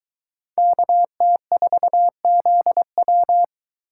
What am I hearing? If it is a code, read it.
Morse code, heard as a keyed tone, about 23 wpm, 700 Hz. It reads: KT4ZW